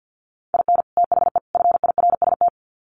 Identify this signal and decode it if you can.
Morse code, heard as a keyed tone, about 50 wpm, 700 Hz. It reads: IN THE FIRST
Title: Sets of 3 Words 50wpm